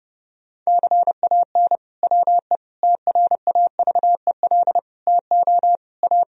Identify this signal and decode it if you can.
Morse code, heard as a keyed tone, about 30 wpm, 700 Hz. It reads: CAN WE TRAVEL TO A
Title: Sentences from Top 1000 Words - Review Part 2 - 30wpm